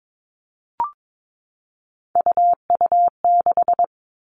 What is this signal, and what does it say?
Morse code, heard as a keyed tone, about 22 wpm, 700 Hz. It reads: UU6
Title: Three Challenging Letter-Number Combinations 22wpm